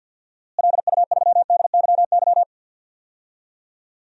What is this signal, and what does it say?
Morse code, heard as a keyed tone, about 52 wpm, 700 Hz. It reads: LU3DYY
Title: DXCC-100 Call Signs - 6 Characters - Repeated 2x - 50wpm